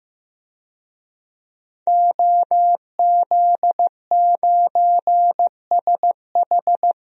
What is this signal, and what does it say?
Morse code, heard as a keyed tone, about 15 wpm, 700 Hz. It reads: OZ9SH
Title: DXCC-100 Call Signs - 5 Characters - 15wpm